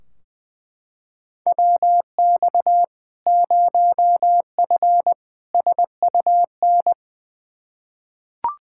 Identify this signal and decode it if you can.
Morse code, heard as a keyed tone, about 20 wpm, 700 Hz. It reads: WX 0F SUN
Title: Practice QSO - Exchange 2 - Info - 20wpm